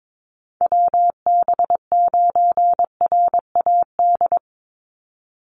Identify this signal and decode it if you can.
Morse code, heard as a keyed tone, about 22 wpm, 700 Hz. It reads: WB9RAD
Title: USA Call Signs 22wpm